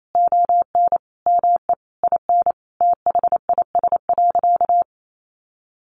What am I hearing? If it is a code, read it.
Morse code, heard as a keyed tone, about 28 wpm, 700 Hz. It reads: ON ME IN THIS.
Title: Sentences from Top 1200 Words 28wpm